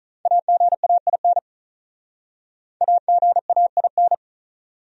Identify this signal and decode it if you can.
Morse code, heard as a keyed tone, about 47 wpm, 700 Hz. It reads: AGAIN AGAIN